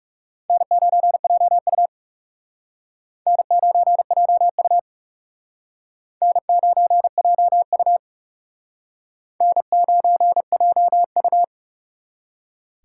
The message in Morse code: N9JU N9JU N9JU N9JU

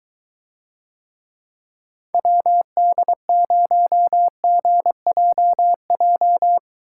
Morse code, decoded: WD0GJJ

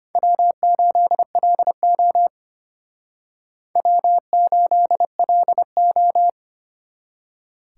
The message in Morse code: W8LO W8LO

